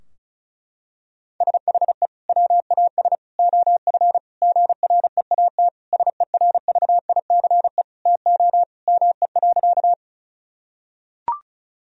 Morse code, heard as a keyed tone, about 34 wpm, 700 Hz. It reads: SHE WAS OF GREAT SERVICE TO ME.
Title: Sentences from Top 1000 Words - Review Part 4 - 35wpm